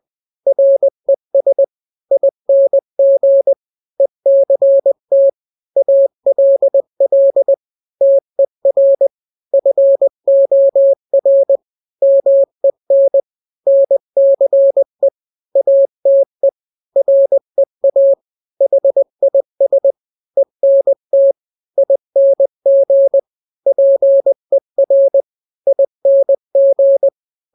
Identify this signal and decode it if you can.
Morse code, heard as a keyed tone, about 20 wpm, 550 Hz. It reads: RES ING ECT ALL TER FOR MEN NCE ATE REA HIS ENT ING PER ING
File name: n-grams / 3-grams 20 18 WPM